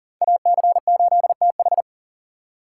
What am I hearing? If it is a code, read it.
Morse code, heard as a keyed tone, about 40 wpm, 700 Hz. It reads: AC8TH
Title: USA Call Signs - 5 Characters - 2x Character Spacing - 40wpm